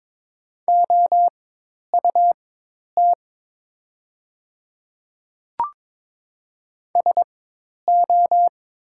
Morse code, heard as a keyed tone, about 22 wpm, spaced as for 10 wpm, 700 Hz. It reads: OUT SO
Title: 4x Character Spacing - Top 100 Words - 22wpm